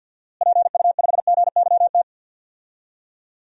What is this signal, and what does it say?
Morse code, heard as a keyed tone, about 52 wpm, 700 Hz. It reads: PU5GYT
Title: DXCC-100 Call Signs - 6 Characters - Repeated 2x - 50wpm